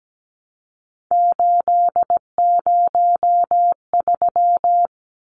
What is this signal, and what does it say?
Morse code, heard as a keyed tone, about 17 wpm, 700 Hz. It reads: 803
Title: Numbers - 3-Digit 17wpm